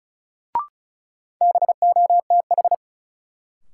Morse code, heard as a keyed tone, about 34 wpm, 700 Hz. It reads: BOTH
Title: Sentences from Top 300 Words - New Word Drill - 35wpm